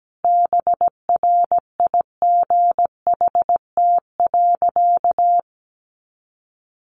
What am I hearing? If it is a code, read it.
Morse code, heard as a keyed tone, about 17 wpm, 700 Hz. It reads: BRIGHT.
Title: Sentences from Top 1500 Words 17wpm